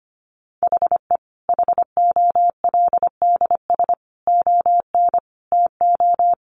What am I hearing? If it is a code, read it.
Morse code, heard as a keyed tone, about 25 wpm, 700 Hz. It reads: HE HOLDS ON TO
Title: Additional Easy Phrases 25wpm